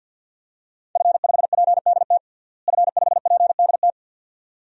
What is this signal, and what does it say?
Morse code, heard as a keyed tone, about 52 wpm, 700 Hz. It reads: F5PDT F5PDT
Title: DXCC-100 Call Signs - 5 Characters - Repeated 2x - 50wpm